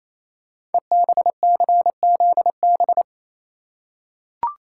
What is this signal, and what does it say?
Morse code, heard as a keyed tone, about 28 wpm, 700 Hz. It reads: EBCZB
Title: Code Groups - 5 Letters - 28wpm